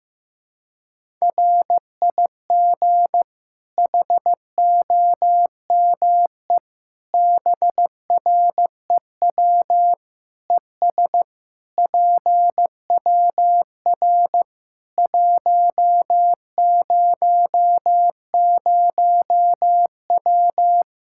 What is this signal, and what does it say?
Morse code, heard as a keyed tone, about 15 wpm, 700 Hz. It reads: RIG HOME BREW ES PWR 100W
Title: Practice QSO - Exchange 2 - Info - 15wpm